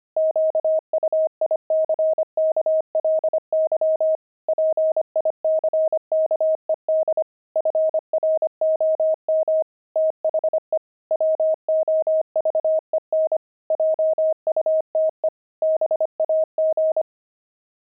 Morse code, heard as a keyed tone, about 25 wpm, 630 Hz. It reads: QUICKLY PICKED FROM THE WOVEN JUTE BAG
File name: pangram-fs-25-cs-25-sixty